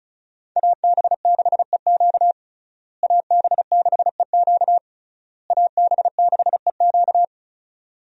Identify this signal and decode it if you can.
Morse code, heard as a keyed tone, about 34 wpm, 700 Hz. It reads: AB6EQ AB6EQ AB6EQ
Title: USA Call Signs - 5 Characters - Repeated 3x - 35wpm